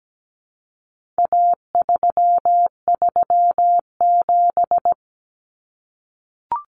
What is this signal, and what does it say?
Morse code, heard as a keyed tone, about 17 wpm, 700 Hz. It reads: A337
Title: Common Cut Numbers - 4-Digit 17wpm